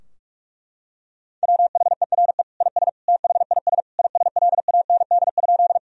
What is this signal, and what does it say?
Morse code, heard as a keyed tone, about 46 wpm, 700 Hz. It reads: WHERE IS THIS ISLAND?
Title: Sentences from Top 1000 Words - Review Part 3 - 45wpm